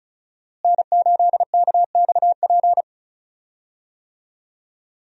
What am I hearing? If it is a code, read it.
Morse code, heard as a keyed tone, about 36 wpm, 700 Hz. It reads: N8KXP
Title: USA Call Signs - 5 Characters - Repeated 3x - 35wpm